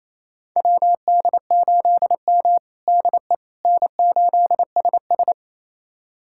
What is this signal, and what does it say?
Morse code, heard as a keyed tone, about 28 wpm, 700 Hz. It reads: WD8M DE N8SS
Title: USA Callsign DE Callsign - 4 Characters - 28wpm